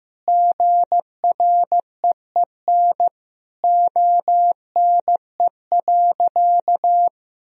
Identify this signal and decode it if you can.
Morse code, heard as a keyed tone, about 15 wpm, 700 Hz. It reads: GREEN ONE.
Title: Sentences from Top 1000 Words - Review Part 2 - 15wpm